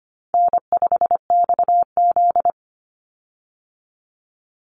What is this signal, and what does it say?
Morse code, heard as a keyed tone, about 25 wpm, 700 Hz. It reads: N5XZ
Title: Easy Call Signs Repeated 3x - 25wpm